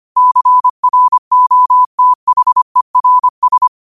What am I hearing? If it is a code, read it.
Morse code, heard as a keyed tone, about 25 wpm, 1000 Hz. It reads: CROTHERS